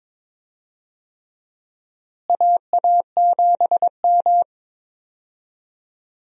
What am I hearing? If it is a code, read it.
Morse code, heard as a keyed tone, about 21 wpm, 700 Hz. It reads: AA7M
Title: USA Call Signs - 4 Characters - Repeated 3x - 22wpm